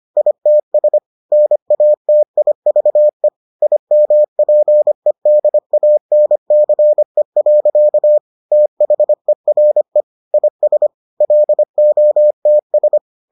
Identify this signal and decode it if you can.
Morse code, heard as a keyed tone, about 25 wpm, 600 Hz. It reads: ITS NATIVE IMPEDANCE. THERE IS LOTS